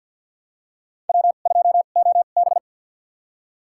Morse code, heard as a keyed tone, about 52 wpm, spaced as for 37 wpm, 700 Hz. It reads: W2OB
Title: USA Call Signs - 4 Characters - 2x Character Spacing - 50wpm